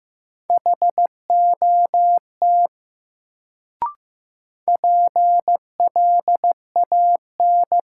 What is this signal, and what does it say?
Morse code, heard as a keyed tone, about 15 wpm, 700 Hz. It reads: HOT PLAN
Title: Sentences from Top 500 Words - New Word Drill - 15wpm